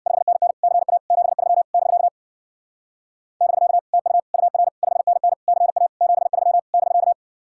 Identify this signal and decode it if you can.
Morse code, heard as a keyed tone, about 64 wpm, 700 Hz. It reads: 5NN CA 73 <BK> <BK> TU UR 5NN CA 73 <BK>